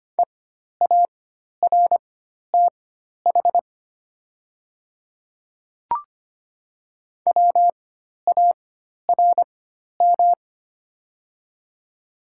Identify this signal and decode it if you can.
Morse code, heard as a keyed tone, about 25 wpm, spaced as for 12 wpm, 690 Hz. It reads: EARTH WARM
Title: 4x Character Spacing - Top 500 Words - 25wpm